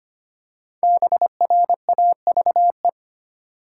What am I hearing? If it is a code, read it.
Morse code, heard as a keyed tone, about 25 wpm, 700 Hz. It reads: BRAVE
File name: 5 Letter Words - First 3 Letters Given - 25wpm